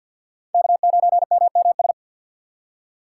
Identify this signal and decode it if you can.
Morse code, heard as a keyed tone, about 52 wpm, 690 Hz. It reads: K9MMS
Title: K1USN SST QSO Practice - 50wpm